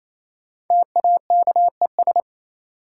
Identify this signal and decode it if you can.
Morse code, heard as a keyed tone, about 28 wpm, 700 Hz. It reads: TAKES
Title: Sets of 5 Words Encore - New Word Drill - 28wpm